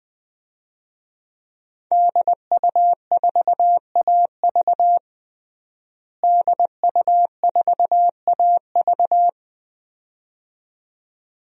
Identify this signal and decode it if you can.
Morse code, heard as a keyed tone, about 21 wpm, 700 Hz. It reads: DU4AV DU4AV
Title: DXCC-100 Call Signs - 5 Characters - Repeated 2x - 20wpm